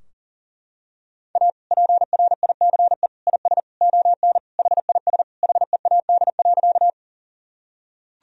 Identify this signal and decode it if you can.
Morse code, heard as a keyed tone, about 40 wpm, 700 Hz. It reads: A PRICE IS ON HIS HEAD.
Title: Sentences from Top 1000 Words - Review Part 4 - 40wpm